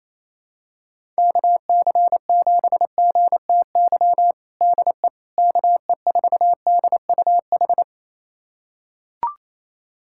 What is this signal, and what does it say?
Morse code, heard as a keyed tone, about 27 wpm, 700 Hz. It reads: KC7GTY DE KE4DUH
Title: USA Callsign DE Callsign - 6 Characters - 28wpm